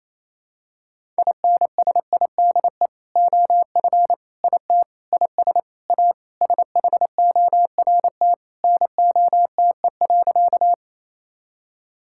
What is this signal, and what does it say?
Morse code, heard as a keyed tone, about 28 wpm, 700 Hz. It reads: INSIDE OF IT IS A SHORT NOTE.
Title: Sentences from Top 1000 Words - Review Part 6 - 28wpm